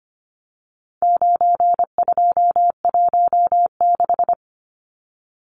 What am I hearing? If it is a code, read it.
Morse code, heard as a keyed tone, about 25 wpm, 700 Hz. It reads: 9216
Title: Numbers - 4-Digit 25wpm